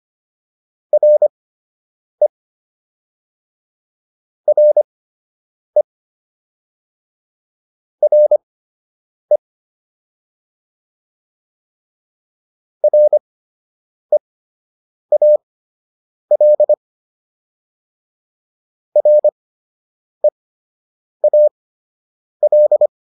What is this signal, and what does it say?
Morse code, heard as a keyed tone, about 25 wpm, spaced as for 8 wpm, 600 Hz. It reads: RE RE RE REAL REAL